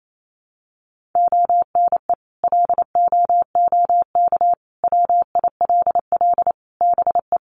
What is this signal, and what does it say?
Morse code, heard as a keyed tone, about 28 wpm, 700 Hz. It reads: ONE LOOK WILL BE
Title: Sentences from Top 1400 Words 28wpm